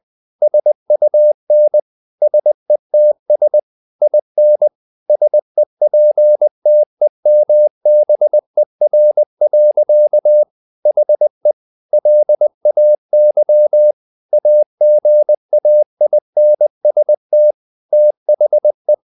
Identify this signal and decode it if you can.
Morse code, heard as a keyed tone, about 20 wpm, 600 Hz. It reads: SUN SETS IN SEPTEMBER. HE LAY AGAINST THE
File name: the old man and the sea part 2 20wpm